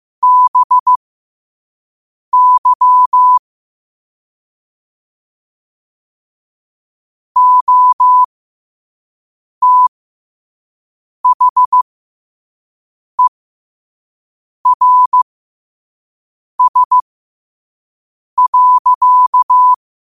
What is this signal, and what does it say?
Morse code, heard as a keyed tone, about 15 wpm, spaced as for 5 wpm, 1000 Hz. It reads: BY OTHERS.
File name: qotdfiles / 5wpm